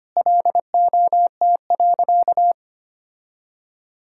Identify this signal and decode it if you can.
Morse code, heard as a keyed tone, about 25 wpm, 700 Hz. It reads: LOT.
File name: practice / 25wpm / Sentences with 3-Word Phrases - 25wpm